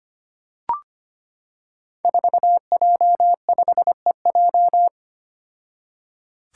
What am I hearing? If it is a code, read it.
Morse code, heard as a keyed tone, about 25 wpm, 700 Hz. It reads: 4J5EJ
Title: Five Challenging Letter-Number Combinations 25wpm